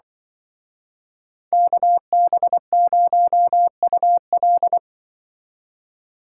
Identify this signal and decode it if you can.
Morse code, heard as a keyed tone, about 24 wpm, 700 Hz. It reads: KB0UL
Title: Speed Racing - USA 2x2 Call Signs 20wpm